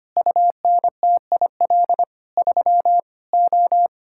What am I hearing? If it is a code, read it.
Morse code, heard as a keyed tone, about 25 wpm, 700 Hz. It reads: UNTIL 3 O
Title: Pangram Challenge v2 - 25wpm